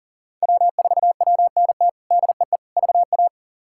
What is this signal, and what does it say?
Morse code, heard as a keyed tone, about 40 wpm, 700 Hz. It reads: W4WNT DEE VA
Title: K1USN SST QSO Practice - 40wpm